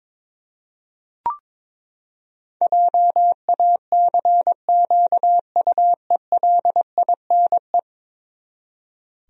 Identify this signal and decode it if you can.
Morse code, heard as a keyed tone, about 22 wpm, 700 Hz. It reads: JACQUELINE
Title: USA Common Names - 22wpm